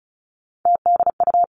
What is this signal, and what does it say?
Morse code, heard as a keyed tone, about 36 wpm, 700 Hz. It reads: TDU
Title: Three Challenging Letter-Number Combinations 35wpm